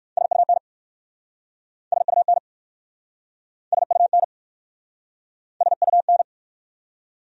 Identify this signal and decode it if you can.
Morse code, heard as a keyed tone, about 59 wpm, 700 Hz. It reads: SUN SUN SUN SUN